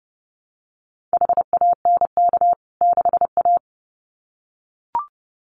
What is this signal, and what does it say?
Morse code, heard as a keyed tone, about 30 wpm, 700 Hz. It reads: HANK 6A
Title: CWT Exchange 30wpm